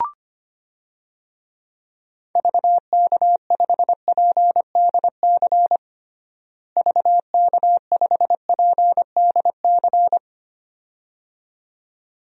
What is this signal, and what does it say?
Morse code, heard as a keyed tone, about 25 wpm, 700 Hz. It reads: VK5PDC VK5PDC
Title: DXCC-100 Call Signs - 6 Characters - Repeated 2x - 25wpm